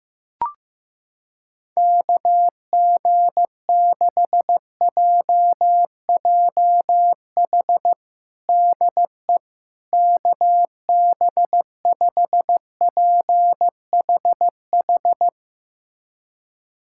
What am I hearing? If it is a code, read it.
Morse code, heard as a keyed tone, about 16 wpm, 700 Hz. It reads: KG6JJH DE KB5PHH
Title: USA Callsign DE Callsign 15wpm